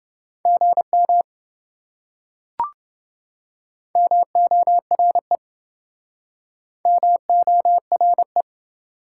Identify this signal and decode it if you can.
Morse code, heard as a keyed tone, about 28 wpm, 700 Hz. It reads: GM MORE MORE